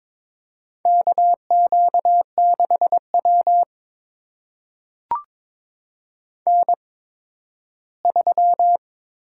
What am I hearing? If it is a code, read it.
Morse code, heard as a keyed tone, about 22 wpm, spaced as for 13 wpm, 700 Hz. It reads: KQ6W N3
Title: USA Call Signs - 4 Characters - 8x Character Spacing - 22wpm